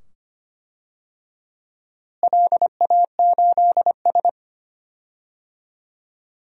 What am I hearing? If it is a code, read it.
Morse code, heard as a keyed tone, about 25 wpm, 700 Hz. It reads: LA8S
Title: DXCC-100 Call Signs - 4 Characters - Repeated 2x - 25wpm